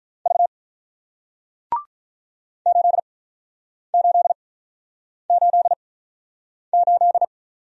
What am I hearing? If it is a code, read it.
Morse code, heard as a keyed tone, about 50 wpm, spaced as for 9 wpm, 700 Hz. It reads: V 8888